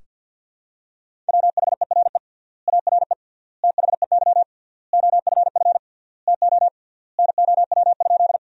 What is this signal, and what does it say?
Morse code, heard as a keyed tone, about 47 wpm, 700 Hz. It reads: WHERE ARE THEY OFF TO NOW?
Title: 3x Word-Spacing - Sentences from Top 200 Words - 50wpm